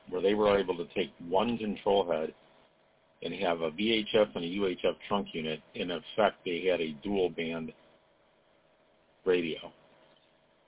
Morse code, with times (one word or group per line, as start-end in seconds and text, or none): none